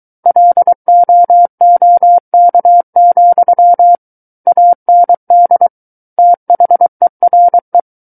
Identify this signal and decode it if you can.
Morse code, heard as a keyed tone, about 23 wpm, 700 Hz. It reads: LOOK, AND THERE
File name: Book One / 14-Chapter 14 - 23 WPM